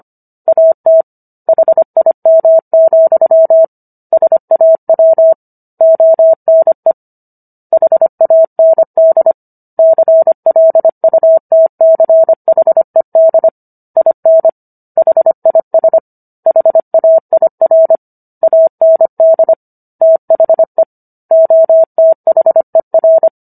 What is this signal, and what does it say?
Morse code, heard as a keyed tone, about 25 wpm, 650 Hz. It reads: AT HIM, SAW ONE HAND CLUTCHED IN HIS HAIR AND THE OTHER